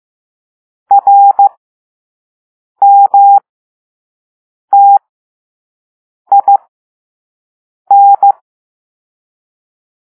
Morse code, heard as a keyed tone, about 15 wpm, spaced as for 5 wpm, 800 Hz. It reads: RMTIN
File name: Morse / Level2-All-A